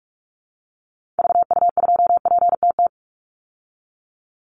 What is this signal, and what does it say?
Morse code, heard as a keyed tone, about 44 wpm, 700 Hz. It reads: VU2PTT
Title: Call Signs Repeated 2x - 45wpm